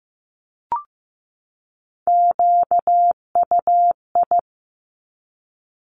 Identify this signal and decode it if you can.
Morse code, heard as a keyed tone, about 15 wpm, 700 Hz. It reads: QUI
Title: Top Three Letter Combinations in Words 15wpm